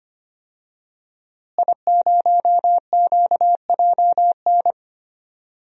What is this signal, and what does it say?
Morse code, heard as a keyed tone, about 25 wpm, 700 Hz. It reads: I0QJN
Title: DXCC-100 Call Signs - 5 Characters - Repeated 3x - 25wpm